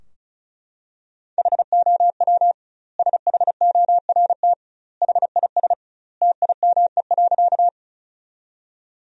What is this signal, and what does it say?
Morse code, heard as a keyed tone, about 36 wpm, 700 Hz. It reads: HOW SHORT HIS TIME.
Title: 2x Word-Spacing - Sentences from Top 400 Words - 35wpm